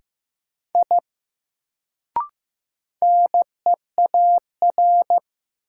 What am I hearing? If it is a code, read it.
Morse code, heard as a keyed tone, about 15 wpm, 700 Hz. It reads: I NEAR
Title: Sentences from Top 200 Words - New Word Drill - 15wpm